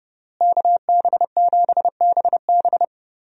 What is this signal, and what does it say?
Morse code, heard as a keyed tone, about 30 wpm, 700 Hz. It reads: KB7BB
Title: USA Call Signs - 5 Characters - Repeated 2x - 30wpm